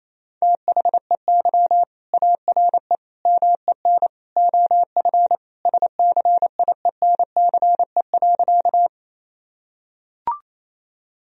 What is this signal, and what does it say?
Morse code, heard as a keyed tone, about 27 wpm, 700 Hz. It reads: THEY ARE MEN OF SCIENCE.
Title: Sentences from Top 1000 Words - Review Part 2 - 28wpm